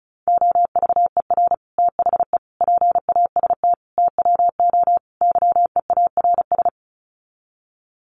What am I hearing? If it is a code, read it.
Morse code, heard as a keyed tone, about 34 wpm, 700 Hz. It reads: OVER THE PAST TWO YEARS
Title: Sets of 5 Words Encore 35wpm